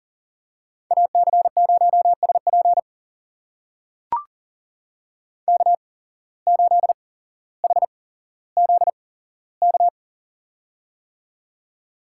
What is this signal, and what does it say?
Morse code, heard as a keyed tone, about 37 wpm, spaced as for 17 wpm, 700 Hz. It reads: AC0SP K8HZK
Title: USA Call Signs - 8x Character Spacing - 40wpm